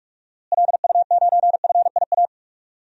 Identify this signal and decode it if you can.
Morse code, heard as a keyed tone, about 46 wpm, 700 Hz. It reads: LU9FIA
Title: DXCC-100 Call Signs - 6 Characters - Repeated 3x - 45wpm